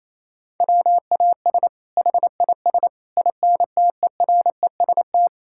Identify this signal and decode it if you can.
Morse code, heard as a keyed tone, about 28 wpm, 700 Hz. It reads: WAS HIS INTEREST
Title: Sentences from Top 1000 Words - Review Part 5 - 28wpm